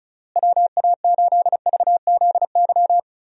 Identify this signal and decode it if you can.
Morse code, heard as a keyed tone, about 36 wpm, 700 Hz. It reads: WA8VZY
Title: USA Call Signs - 6 Characters - Repeated 3x - 35wpm